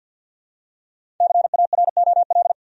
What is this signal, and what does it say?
Morse code, heard as a keyed tone, about 52 wpm, 700 Hz. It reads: CAROL
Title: USA Common Names - Max 5 Letters - 50wpm